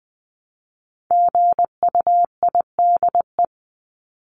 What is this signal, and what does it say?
Morse code, heard as a keyed tone, about 20 wpm, 700 Hz. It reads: GUIDE
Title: Top 501-750 Words 20wpm